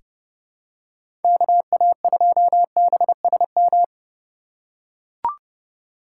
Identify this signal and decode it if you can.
Morse code, heard as a keyed tone, about 31 wpm, 700 Hz. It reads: KA2BSM
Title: USA Call Signs - 3x Character Spacing - 30wpm